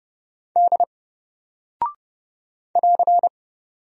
Morse code, heard as a keyed tone, about 30 wpm, 700 Hz. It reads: D <AR>
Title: Single Character (Letter:Number:Punctuation:Prosign) Alt 30wpm